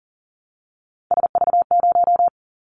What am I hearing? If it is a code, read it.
Morse code, heard as a keyed tone, about 40 wpm, 700 Hz. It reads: SV0
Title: Callsign Prefixes for Top 50 DXCC - 40wpm